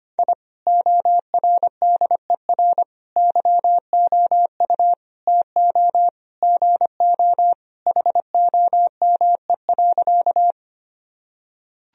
Morse code, heard as a keyed tone, about 25 wpm, 700 Hz. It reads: I ORDER YOU TO GO HOME.
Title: Sentences from Top 1000 Words - Review Part 2 - 25wpm